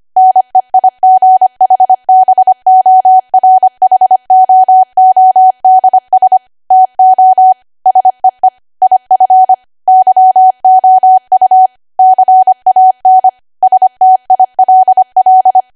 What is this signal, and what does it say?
Morse code, heard as a keyed tone, about 25 wpm, 750 Hz. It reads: NEIGHBORHOODS TO SEE IF YOU CAN STILL
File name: Archive / 25 WPM / 25WPM